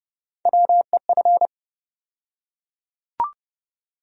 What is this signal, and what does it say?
Morse code, heard as a keyed tone, about 30 wpm, 700 Hz. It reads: WEF
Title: USA Call Sign Postfixes - 3 Letters 30wpm